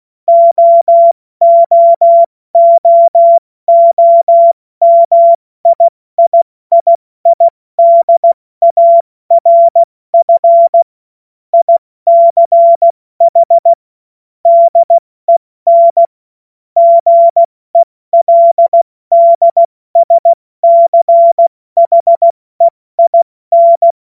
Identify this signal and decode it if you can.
Morse code, heard as a keyed tone, about 16 wpm, 680 Hz. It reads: OOOOMIIII DARF ICH DEN GELDSCHEIN
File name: Morsekurs / Witze / Witz 17. August 2021, 70 F80